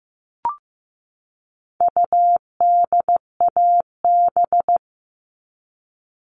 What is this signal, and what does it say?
Morse code, heard as a keyed tone, about 15 wpm, 700 Hz. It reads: UDAB
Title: All Cut Numbers - 4-Digit 15wpm